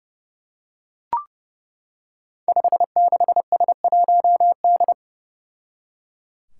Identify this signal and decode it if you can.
Morse code, heard as a keyed tone, about 30 wpm, 700 Hz. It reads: 56S1D
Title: Five Challenging Letter-Number Combinations 30wpm